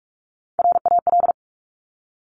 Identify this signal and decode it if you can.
Morse code, heard as a keyed tone, about 46 wpm, 700 Hz. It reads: RAL
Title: Top Three Letter Combinations in Words 45wpm